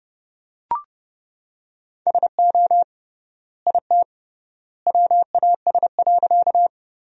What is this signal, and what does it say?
Morse code, heard as a keyed tone, about 32 wpm, 700 Hz. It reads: SO IT WAS.
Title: 3x Word-Spacing - Sentences from Top 100 Words - 30wpm